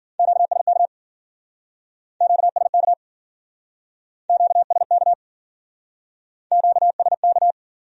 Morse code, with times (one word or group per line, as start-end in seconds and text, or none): 0.2-0.9 QSK
2.2-2.9 QSK
4.3-5.1 QSK
6.5-7.5 QSK